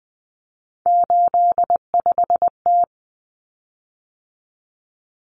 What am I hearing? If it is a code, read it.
Morse code, heard as a keyed tone, about 20 wpm, 700 Hz. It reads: 85T
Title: Common Cut Numbers - 3-Digit 20wpm